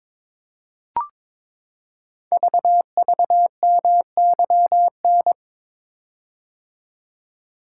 Morse code, heard as a keyed tone, about 22 wpm, 700 Hz. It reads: VVMYN
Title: Code Groups - 5 Letters - 22wpm